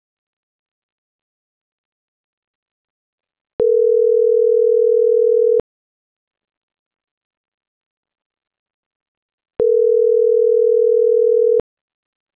Ringback tone (North America), 440 + 480 Hz.
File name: CENTRAL NAVIERAS / out roberto.rueda